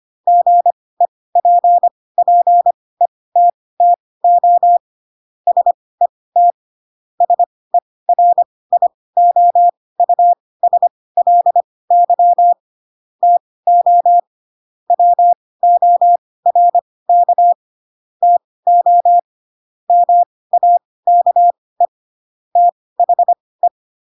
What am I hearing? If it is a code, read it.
Morse code, heard as a keyed tone, about 25 wpm, spaced as for 18 wpm, 700 Hz. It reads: GEPPETTO SET SERIOUSLY TO WORK TO MAKE THE
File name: Pinocchio ch3 12 25 ik1hge